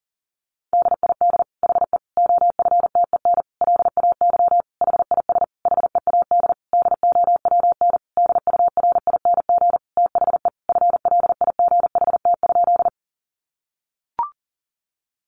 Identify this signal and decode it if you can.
Morse code, heard as a keyed tone, about 40 wpm, 700 Hz. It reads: DID HE OFTEN LAY HIS HEAD DOWN DURING THE FLIGHT?
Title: Sentences from Top 1000 Words 40wpm